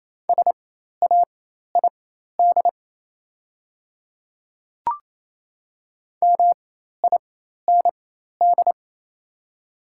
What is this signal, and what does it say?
Morse code, heard as a keyed tone, about 28 wpm, spaced as for 13 wpm, 700 Hz. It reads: SAID MIND
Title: 4x Character Spacing - Top 500 Words - 28wpm